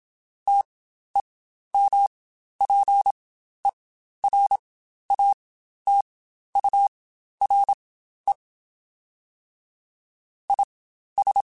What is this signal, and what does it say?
Morse code, heard as a keyed tone, about 27 wpm, spaced as for 12 wpm, 800 Hz. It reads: TEMPERATURE IS
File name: Spaced Characters / Spaced Characters-mp3s / Weather 5 12 sp @25